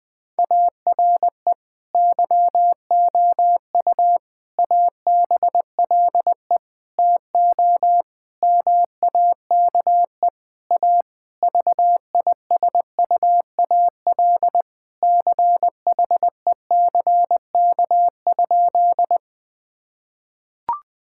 Morse code, heard as a keyed tone, about 20 wpm, 700 Hz. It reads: ARE YOU ABLE TO MAKE A VISUAL CHECK?
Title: Sentences from Top 1000 Words - Review Part 6 - 20wpm